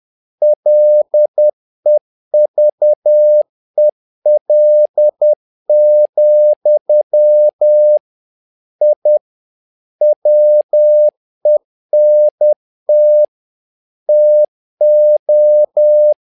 Morse code, heard as a keyed tone, about 10 wpm, 600 Hz. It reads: LEVEL, I WENT TO
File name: learning center / tower trouble / tower trouble 10wpm